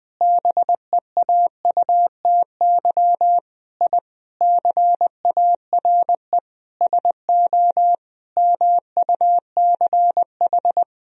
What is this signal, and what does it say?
Morse code, heard as a keyed tone, about 20 wpm, 700 Hz. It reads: BEAUTY I CARE SO MUCH
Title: Sentences from Top 1000 Words - Review Part 2 - 20wpm